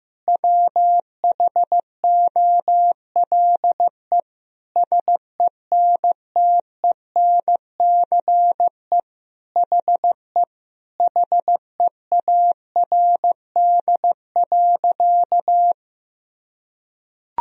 Morse code, heard as a keyed tone, about 15 wpm, 700 Hz. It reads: WHOLE SENTENCE HE HEARD.
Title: Sentences from Top 1000 Words - Review Part 1 - 15wpm